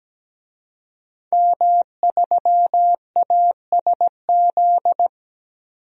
Morse code, heard as a keyed tone, about 17 wpm, 700 Hz. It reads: M3ASZ
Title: DXCC-100 Call Signs - 5 Characters - 17wpm